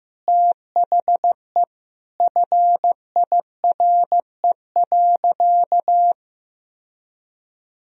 Morse code, heard as a keyed tone, about 15 wpm, 700 Hz. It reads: THE FIRE.
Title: Sentences from Top 1000 Words - Review Part 2 - 15wpm